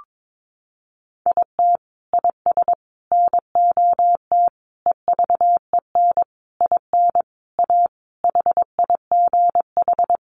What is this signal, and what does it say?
Morse code, heard as a keyed tone, about 22 wpm, 700 Hz. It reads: IT IS NOT EVEN IN A HIGH
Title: Sentences from Top 1500 Words 22wpm